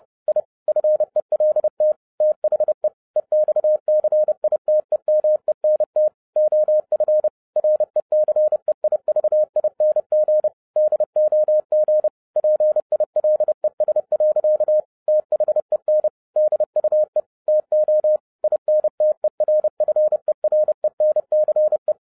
Morse code, heard as a keyed tone, about 30 wpm, 610 Hz. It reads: I FELT THE EXCITEMENT OF RECEIVING DOG PILES. THEN DUE TO INTERFERENCE